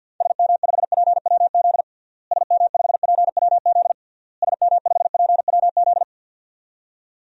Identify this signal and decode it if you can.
Morse code, heard as a keyed tone, about 48 wpm, 700 Hz. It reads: SM5PWZ SM5PWZ SM5PWZ
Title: DXCC-100 Call Signs - 6 Characters - Repeated 3x - 50wpm